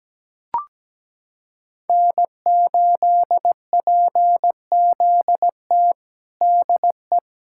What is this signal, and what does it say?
Morse code, heard as a keyed tone, about 17 wpm, 700 Hz. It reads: N8PZT DE
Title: USA Callsign DE Callsign - 5 Characters - 17wpm